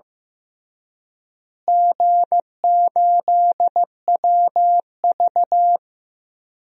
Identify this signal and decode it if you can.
Morse code, heard as a keyed tone, about 15 wpm, 700 Hz. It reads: G8WV
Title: DXCC-100 Call Signs - 4 Characters - Repeated 3x - 15wpm